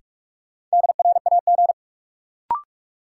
Morse code, heard as a keyed tone, about 45 wpm, 700 Hz. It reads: DRAG
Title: Sentences from Top 1500 Words - New Word Drill - 45wpm